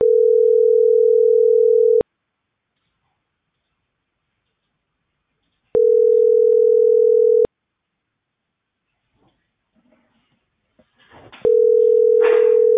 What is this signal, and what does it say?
Ringback tone (North America), 440 + 480 Hz.